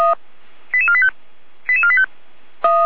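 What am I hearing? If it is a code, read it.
Five-tone selective call (ZVEI), heard as quick runs of five tones, tones of about 70 ms. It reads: 88485 88485
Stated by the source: SDR-received ZVEI five-tone selective call, code 88125.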